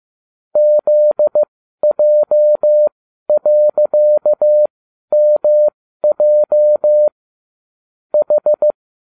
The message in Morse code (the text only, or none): ZJ.MJ H